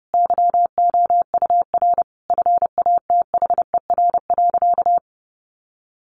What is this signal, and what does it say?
Morse code, heard as a keyed tone, about 30 wpm, 700 Hz. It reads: YOUR FATHER.
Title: Sentences from Top 1400 Words 30wpm